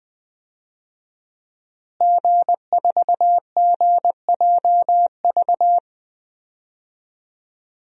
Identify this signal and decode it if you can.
Morse code, heard as a keyed tone, about 20 wpm, 700 Hz. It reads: G4GJV
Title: DXCC-100 Call Signs - 5 Characters - Repeated 3x - 20wpm